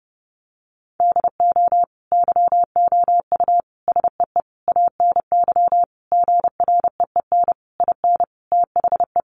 Morse code, heard as a keyed tone, about 28 wpm, 700 Hz. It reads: DO YOU SEE ANY GREEN IN THE
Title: Sentences from Top 500 Words 30wpm